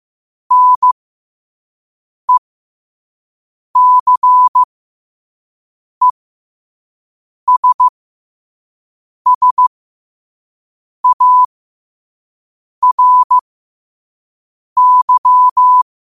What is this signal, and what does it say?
Morse code, heard as a keyed tone, about 15 wpm, spaced as for 5 wpm, 1000 Hz. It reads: NECESSARY
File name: qotdfiles / 5wpm